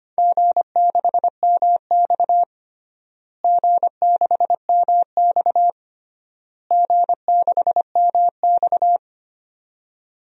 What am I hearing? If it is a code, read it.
Morse code, heard as a keyed tone, about 25 wpm, 700 Hz. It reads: G6MX G6MX G6MX
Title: DXCC-100 Call Signs - 4 Characters - Repeated 3x - 25wpm